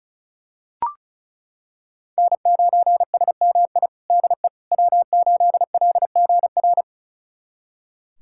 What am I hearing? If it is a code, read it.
Morse code, heard as a keyed tone, about 36 wpm, 700 Hz. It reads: N9SMI DE W8LGR
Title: USA Callsign DE Callsign - 5 Characters - 35wpm